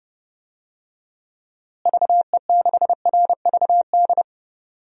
Morse code, heard as a keyed tone, about 31 wpm, 700 Hz. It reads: VE6RVD
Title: DXCC-100 Call Signs - 6 Characters - Repeated 3x - 30wpm